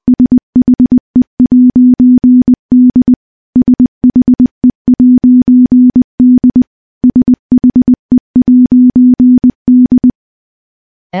Morse code, heard as a keyed tone, about 21 wpm, 250 Hz. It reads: SHE'D SHE'D SHE'D
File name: morse / morse-training-20WPM